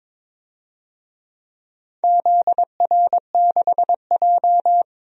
Morse code, heard as a keyed tone, about 22 wpm, 700 Hz. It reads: ZR6J
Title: DXCC-100 Call Signs - 4 Characters - 22wpm